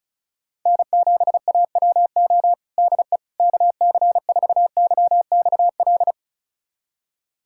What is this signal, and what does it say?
Morse code, heard as a keyed tone, about 36 wpm, 700 Hz. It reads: N7AWO DE KC4YXL
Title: USA Callsign DE Callsign 35wpm